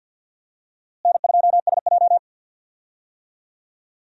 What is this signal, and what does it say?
Morse code, heard as a keyed tone, about 47 wpm, 700 Hz. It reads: N2SJ
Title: USA Call Signs - 4 Characters - Repeated 2x - 50wpm